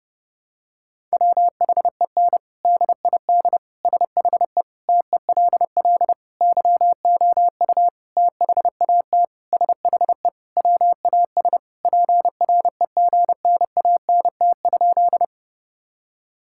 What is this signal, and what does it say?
Morse code, heard as a keyed tone, about 30 wpm, 700 Hz. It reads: WHEN DID SHE TELL YOU THAT SHE WAS PREGNANT?
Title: Sentences from Top 1000 Words - Review Part 6 - 30wpm